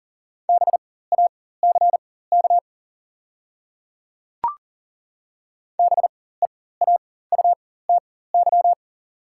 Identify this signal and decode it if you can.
Morse code, heard as a keyed tone, about 39 wpm, spaced as for 19 wpm, 700 Hz. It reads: BACK BEAUTY
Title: 4x Character Spacing - Top 500 Words - 40wpm